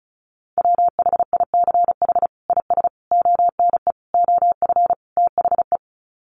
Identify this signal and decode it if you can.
Morse code, heard as a keyed tone, about 34 wpm, 700 Hz. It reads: WHICH IS ONE OF THE
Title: Sets of 5 Words Encore 35wpm